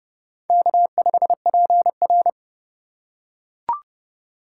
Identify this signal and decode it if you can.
Morse code, heard as a keyed tone, about 32 wpm, 700 Hz. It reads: K5PR
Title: USA Call Signs - 4x Character Spacing - 30wpm